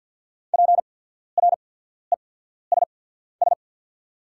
Morse code, heard as a keyed tone, about 48 wpm, spaced as for 14 wpm, 690 Hz. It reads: PRESS
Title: 8x Character Spacing - Top 300 Words - 50wpm